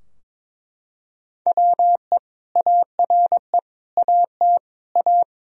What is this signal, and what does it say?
Morse code, heard as a keyed tone, about 21 wpm, 700 Hz. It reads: WE ARE AT A
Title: Sentences from Top 1000 Words - Review Part 2 - 22wpm